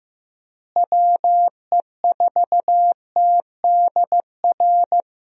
Morse code, heard as a keyed tone, about 15 wpm, 700 Hz. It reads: WE4TDR
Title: USA Call Signs - 6 Characters - Repeated 3x - 15wpm